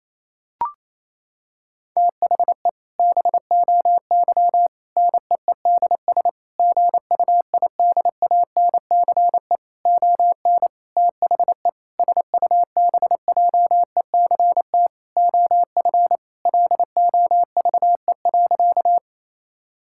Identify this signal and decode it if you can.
Morse code, heard as a keyed tone, about 28 wpm, 700 Hz. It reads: THE BOY NEEDS GUIDANCE ON THE SUBJECT OF LOVE.
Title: Sentences from Top 1600 Words 28wpm